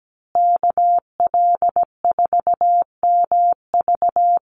Morse code, heard as a keyed tone, about 17 wpm, 700 Hz. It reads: KL4MV
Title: USA Call Signs Repeated 2x 17wpm